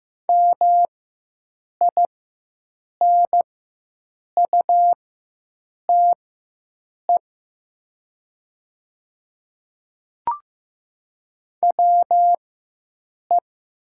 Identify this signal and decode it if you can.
Morse code, heard as a keyed tone, about 15 wpm, spaced as for 7 wpm, 700 Hz. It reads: MINUTE WE
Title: 4x Character Spacing - Top 500 Words - 15wpm